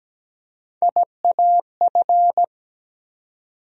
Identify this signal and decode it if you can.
Morse code, heard as a keyed tone, about 17 wpm, 700 Hz. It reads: IAF
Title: USA Call Sign Postfixes - 3 Letters 17wpm